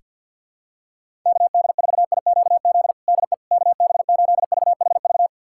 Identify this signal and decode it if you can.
Morse code, heard as a keyed tone, about 52 wpm, 700 Hz. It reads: KD4IQZ DE KB8VHV
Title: USA Callsign DE Callsign - 6 Characters - 50wpm